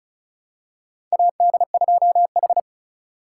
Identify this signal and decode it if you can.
Morse code, heard as a keyed tone, about 36 wpm, 700 Hz. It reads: AD2H
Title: USA Call Signs - 4 Characters - 4x Character Spacing - 35wpm